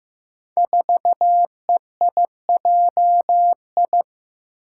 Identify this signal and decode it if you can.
Morse code, heard as a keyed tone, about 15 wpm, 700 Hz. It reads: 4EIJI
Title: Five Challenging Letter-Number Combinations 15wpm